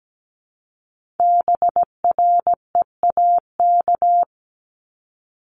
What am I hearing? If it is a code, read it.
Morse code, heard as a keyed tone, about 17 wpm, 700 Hz. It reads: BREAK
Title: Multi-letter Phonemes 17wpm